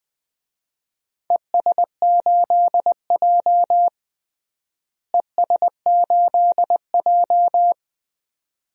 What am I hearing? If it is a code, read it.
Morse code, heard as a keyed tone, about 20 wpm, 700 Hz. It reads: ES8J ES8J
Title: DXCC-100 Call Signs - 4 Characters - Repeated 2x - 20wpm